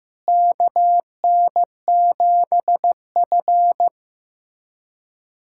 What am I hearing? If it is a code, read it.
Morse code, heard as a keyed tone, about 15 wpm, 700 Hz. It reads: KN7F
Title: USA Callsign DE Callsign - 4 Characters - 15wpm